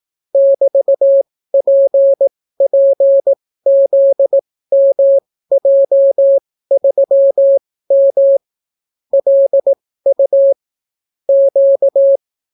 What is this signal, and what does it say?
Morse code, heard as a keyed tone, about 18 wpm, spaced as for 14 wpm, 550 Hz. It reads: <BT>PPZMJ3M LU Q